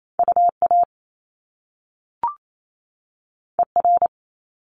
Morse code, heard as a keyed tone, about 28 wpm, 700 Hz. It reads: UA ER
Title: Top Two Letter Combinations in Words 28wpm